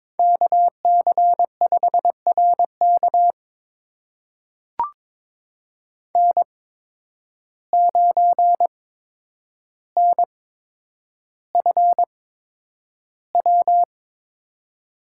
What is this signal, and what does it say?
Morse code, heard as a keyed tone, about 22 wpm, spaced as for 9 wpm, 700 Hz. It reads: KC5RK N9NFW
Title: USA Call Signs - 5 Characters - 8x Character Spacing - 22wpm